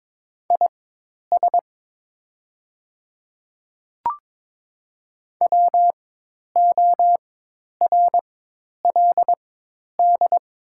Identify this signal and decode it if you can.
Morse code, heard as a keyed tone, about 22 wpm, spaced as for 10 wpm, 690 Hz. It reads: IS WORLD
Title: 4x Character Spacing - Top 200 Words - 22wpm